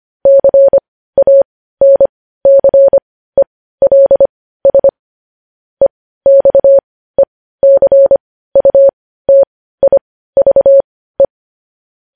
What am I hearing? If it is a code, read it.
Morse code, heard as a keyed tone, about 25 wpm, spaced as for 15 wpm, 550 Hz. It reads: CANCELS EXECUTIVE
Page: news headlines in morse code at 15 wpm